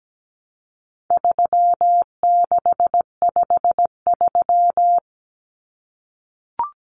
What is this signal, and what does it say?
Morse code, heard as a keyed tone, about 17 wpm, 700 Hz. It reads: 3653
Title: Common Cut Numbers - 4-Digit 17wpm